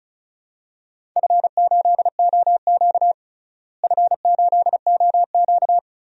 Morse code, heard as a keyed tone, about 36 wpm, 700 Hz. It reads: F8OQ F8OQ
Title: DXCC-100 Call Signs - 4 Characters - Repeated 3x - 35wpm